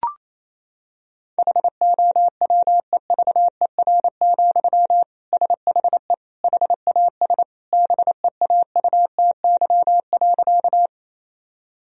Morse code, heard as a keyed tone, about 27 wpm, 700 Hz. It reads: HOWEVER, SHE HAS BEAUTY.
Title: Sentences from Top 1000 Words - Review Part 2 - 28wpm